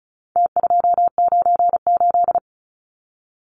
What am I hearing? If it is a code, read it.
Morse code, heard as a keyed tone, about 34 wpm, 700 Hz. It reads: T298
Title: Numbers - Contest Style - 4-Digit 35wpm